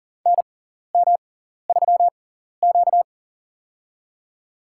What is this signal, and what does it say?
Morse code, heard as a keyed tone, about 39 wpm, spaced as for 14 wpm, 700 Hz. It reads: NM3Q
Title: USA Call Signs - 4 Characters - 6x Character Spacing - 40wpm